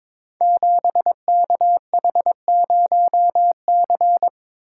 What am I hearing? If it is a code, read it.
Morse code, heard as a keyed tone, about 23 wpm, 700 Hz. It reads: 7KH0C
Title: Code Groups - 5 Letter:Numbers - 22wpm